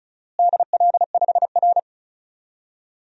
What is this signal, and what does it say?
Morse code, heard as a keyed tone, about 34 wpm, 700 Hz. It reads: DL5R
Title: DXCC-100 Call Signs - 4 Characters - Repeated 3x - 35wpm